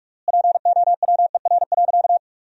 Morse code, heard as a keyed tone, about 46 wpm, 700 Hz. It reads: POWER.
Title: 3x Word-Spacing - Sentences from Top 500 Words - 45wpm